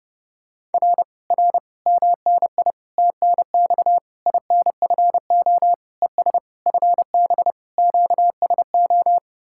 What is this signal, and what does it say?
Morse code, heard as a keyed tone, about 30 wpm, 700 Hz. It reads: R R MNI TNX INFO ES FB QSO